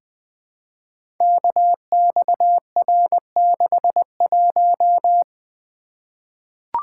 Morse code, heard as a keyed tone, about 21 wpm, 700 Hz. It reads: KXR61
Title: Code Groups - 5 Letter:Numbers - 20wpm